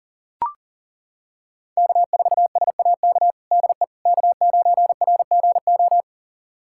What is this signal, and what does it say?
Morse code, heard as a keyed tone, about 40 wpm, 700 Hz. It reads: K4SAK DE K9RGO
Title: USA Callsign DE Callsign - 5 Characters - 40wpm